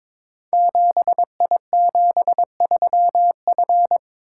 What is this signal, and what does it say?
Morse code, heard as a keyed tone, about 22 wpm, 700 Hz. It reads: 7I73F
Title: Code Groups - 5 Letter:Numbers - 22wpm